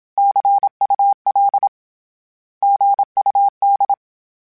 Morse code, heard as a keyed tone, about 27 wpm, 800 Hz. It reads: CUL GUD
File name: Abreviations / Abreviations mp3s / Abreviations@25